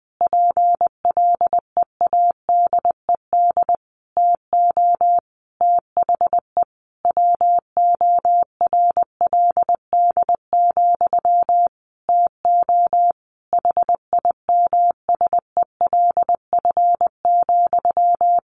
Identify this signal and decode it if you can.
Morse code, heard as a keyed tone, about 20 wpm, 700 Hz. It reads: PLEADED TO THE WORLD, TO HIMSELF,
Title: A Sound Of Thunder 20wpm